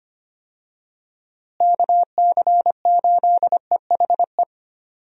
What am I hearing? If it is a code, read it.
Morse code, heard as a keyed tone, about 25 wpm, 700 Hz. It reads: KC8EHE
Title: USA Call Signs - 6 Characters - Repeated 3x - 25wpm